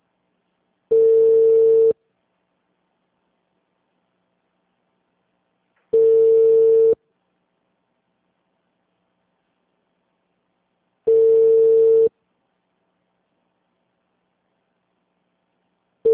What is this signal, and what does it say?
Ringback tone (China, 450 Hz).
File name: out- -2015- - - .180